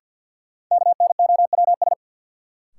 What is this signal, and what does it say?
Morse code, heard as a keyed tone, about 51 wpm, 700 Hz. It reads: KNOWS
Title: Sentences from Top 1000 Words - Review Part 1 - New Word Drill - 50wpm